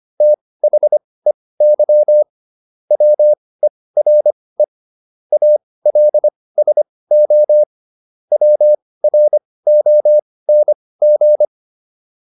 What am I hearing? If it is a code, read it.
Morse code, heard as a keyed tone, about 25 wpm, spaced as for 18 wpm, 600 Hz. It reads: THEY WERE ALSO WRONG